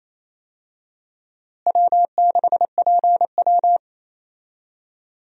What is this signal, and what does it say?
Morse code, heard as a keyed tone, about 28 wpm, 700 Hz. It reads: W6PW
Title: USA Call Signs - 4 Characters - Repeated 2x - 28wpm